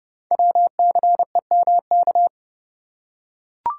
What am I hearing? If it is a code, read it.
Morse code, heard as a keyed tone, about 29 wpm, 700 Hz. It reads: WCEMK
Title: Code Groups - 5 Letters - 30wpm